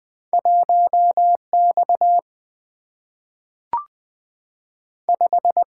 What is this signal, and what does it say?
Morse code, heard as a keyed tone, about 20 wpm, 700 Hz. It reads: 1X 5
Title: Single-Letter Single-Number 20wpm